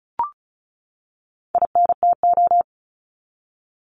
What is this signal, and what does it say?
Morse code, heard as a keyed tone, about 34 wpm, 700 Hz. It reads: INTO
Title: Top 501-750 Words 35wpm